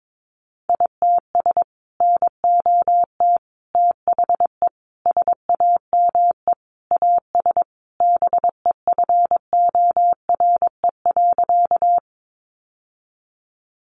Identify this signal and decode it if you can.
Morse code, heard as a keyed tone, about 22 wpm, 700 Hz. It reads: ITS NOT THE SAME AS BEFORE.
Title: Sentences from Top 600 Words 22wpm